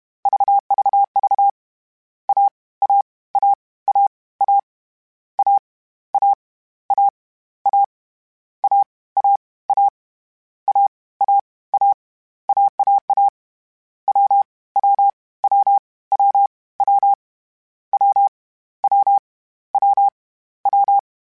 Morse code, heard as a keyed tone, about 31 wpm, 800 Hz. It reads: VVV A A A A A A A A A A A A A A A AAA W W W W W W W W W
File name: Difficult Combos / Difficult Combos-mp3s / AWJ @30